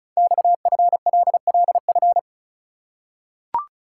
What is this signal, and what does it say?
Morse code, heard as a keyed tone, about 36 wpm, 700 Hz. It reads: XFLLF
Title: Code Groups - 5 Letters - 35wpm